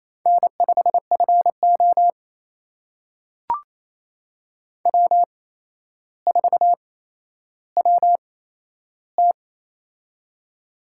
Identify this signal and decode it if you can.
Morse code, heard as a keyed tone, about 28 wpm, spaced as for 12 wpm, 700 Hz. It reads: N5FO W4WT
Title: USA Call Signs - 4 Characters - 8x Character Spacing - 28wpm